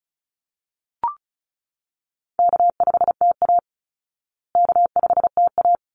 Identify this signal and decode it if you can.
Morse code, heard as a keyed tone, about 34 wpm, 700 Hz. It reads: K5TA K5TA
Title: Easy Call Signs Repeated 2x - 35wpm